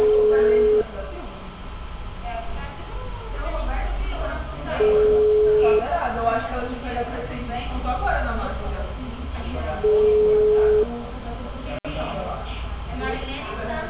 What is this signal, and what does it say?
Ringback tone (Europe, 425 Hz), with speech.